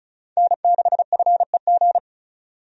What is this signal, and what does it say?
Morse code, heard as a keyed tone, about 36 wpm, 700 Hz. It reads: N6FEG
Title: USA Call Signs - 5 Characters - 35wpm